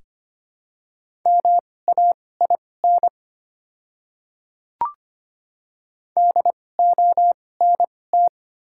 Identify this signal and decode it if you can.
Morse code, heard as a keyed tone, about 25 wpm, spaced as for 18 wpm, 700 Hz. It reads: MAIN DONT
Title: 2x Character Spacing - Top 500 Words - 25wpm